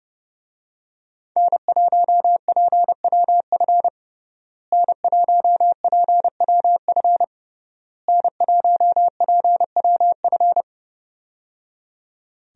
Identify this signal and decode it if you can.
Morse code, heard as a keyed tone, about 29 wpm, 700 Hz. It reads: N1PWF N1PWF N1PWF
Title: USA Call Signs - 5 Characters - Repeated 3x - 30wpm